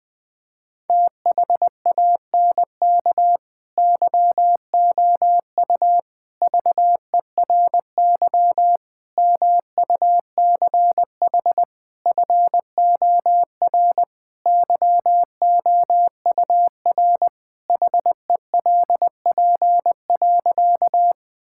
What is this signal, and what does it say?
Morse code, heard as a keyed tone, about 21 wpm, 700 Hz. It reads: THANK YOU VERY MUCH FOR YOUR HELP.
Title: Sentences with 4-Word Phrases - 20wpm